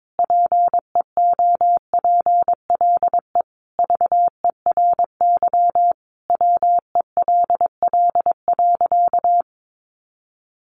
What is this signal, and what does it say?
Morse code, heard as a keyed tone, about 22 wpm, 700 Hz. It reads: PEOPLE VERY WELL.
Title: Sentences from Top 1300 Words 22wpm